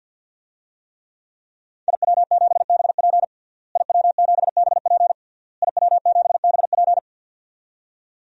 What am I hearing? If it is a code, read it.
Morse code, heard as a keyed tone, about 52 wpm, 700 Hz. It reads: IW7BP IW7BP IW7BP
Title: DXCC-100 Call Signs - 5 Characters - Repeated 3x - 50wpm